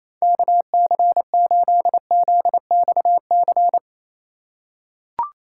Morse code, heard as a keyed tone, about 28 wpm, 700 Hz. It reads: KC8ZXC
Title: USA Call Signs - 4x Character Spacing - 28wpm